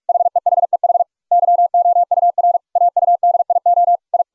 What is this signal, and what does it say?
Morse code, heard as a keyed tone, about 44 wpm, 690 Hz. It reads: HEHEH YOUR AUDIO I